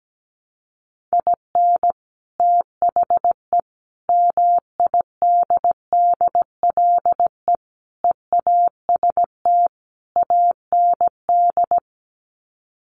Morse code, heard as a keyed tone, about 17 wpm, 700 Hz. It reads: IN THE MIDDLE EAST AND
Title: Sets of 5 Words Encore 17wpm